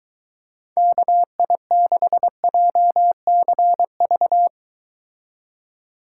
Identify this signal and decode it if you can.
Morse code, heard as a keyed tone, about 23 wpm, 700 Hz. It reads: KI6JCV